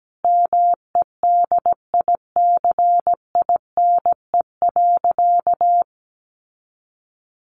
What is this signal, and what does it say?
Morse code, heard as a keyed tone, about 17 wpm, 700 Hz. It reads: MEDICINE.
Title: Sentences from Top 900 Words 17wpm